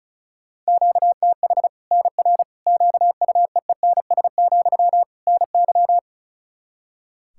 Morse code, heard as a keyed tone, about 34 wpm, 700 Hz. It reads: QTH NR QUEENS, NY
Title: Practice QSO - Exchange 1 - Beginning - 35wpm